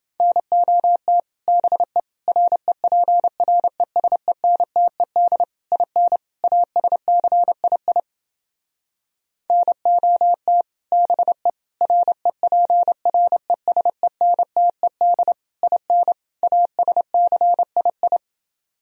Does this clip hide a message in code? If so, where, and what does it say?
Yes: Morse code from 0.2 to 18.2 s, heard as a keyed tone, about 29 wpm, 700 Hz. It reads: NOT BE REPRESENTED IN ASCII NOT BE REPRESENTED IN ASCII